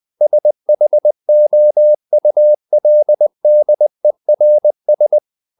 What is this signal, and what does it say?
Morse code, heard as a keyed tone, about 20 wpm, 600 Hz. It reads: SHOULDERS
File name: the old man and the sea / the old man and the sea part 1 20wpm